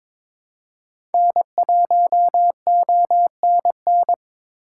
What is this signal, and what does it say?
Morse code, heard as a keyed tone, about 23 wpm, 700 Hz. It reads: N1ONN
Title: USA Call Signs - 6x Character Spacing - 22wpm